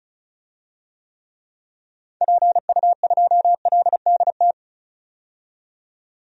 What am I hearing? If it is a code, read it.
Morse code, heard as a keyed tone, about 36 wpm, 700 Hz. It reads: PU2LDT
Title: DXCC-100 Call Signs - 6 Characters - 35wpm